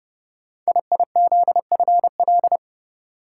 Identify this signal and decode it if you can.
Morse code, heard as a keyed tone, about 30 wpm, 700 Hz. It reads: IIZFL
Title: Code Groups - 5 Letters - 30wpm